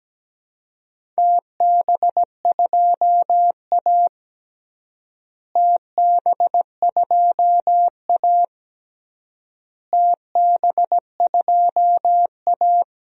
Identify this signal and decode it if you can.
Morse code, heard as a keyed tone, about 17 wpm, 700 Hz. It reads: TB2A TB2A TB2A
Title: DXCC-100 Call Signs - 4 Characters - Repeated 3x - 17wpm